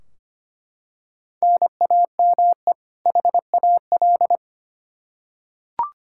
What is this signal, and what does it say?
Morse code, heard as a keyed tone, about 25 wpm, 700 Hz. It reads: NAME HAL
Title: Practice QSO - Exchange 1 - Beginning - 25wpm